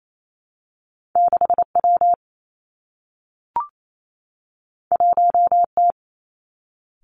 Morse code, heard as a keyed tone, about 28 wpm, 700 Hz. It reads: 6 W 1 T
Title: Two Challenging Letter-Number Combinations 28wpm